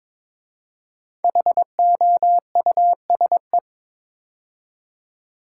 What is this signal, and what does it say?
Morse code, heard as a keyed tone, about 22 wpm, 700 Hz. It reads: HOUSE
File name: practice / 22wpm / 5 Letter Words - First 3 Letters Given - 22wpm